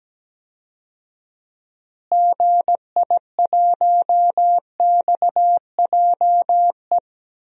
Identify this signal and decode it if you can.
Morse code, heard as a keyed tone, about 17 wpm, 700 Hz. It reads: GI1XJE
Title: DXCC-100 Call Signs - 6 Characters - Repeated 2x - 17wpm